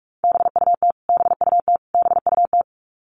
Morse code, heard as a keyed tone, about 44 wpm, 700 Hz. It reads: BUT BUT BUT
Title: Top 200 Words Repeated 45wpm